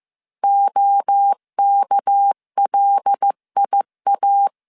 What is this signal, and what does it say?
Morse code, heard as a keyed tone, about 15 wpm, 800 Hz. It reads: OKLIA